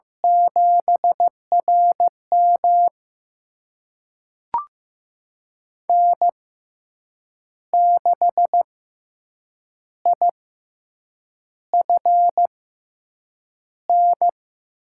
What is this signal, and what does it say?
Morse code, heard as a keyed tone, about 15 wpm, spaced as for 7 wpm, 700 Hz. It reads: 7RM N6IFN
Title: USA Call Signs - 5 Characters - 6x Character Spacing - 15wpm